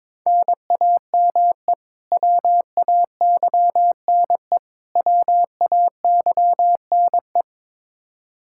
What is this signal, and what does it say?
Morse code, heard as a keyed tone, about 22 wpm, 700 Hz. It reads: NAME WAYNE WAYNE
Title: Practice QSO - Exchange 1 - Beginning - 22wpm